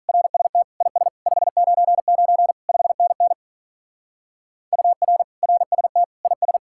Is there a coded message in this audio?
Yes: Morse code from 0.1 to 6.6 s, heard as a keyed tone, about 44 wpm, 700 Hz. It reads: RST IS 599 5NN UR RST IS